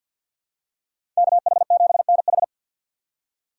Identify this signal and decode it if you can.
Morse code, heard as a keyed tone, about 52 wpm, 700 Hz. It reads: KH7NH
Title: USA Call Signs - 5 Characters - 4x Character Spacing - 50wpm